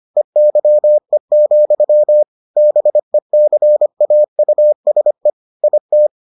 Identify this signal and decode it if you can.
Morse code, heard as a keyed tone, about 25 wpm, 600 Hz. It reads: EYE, BECAUSE IT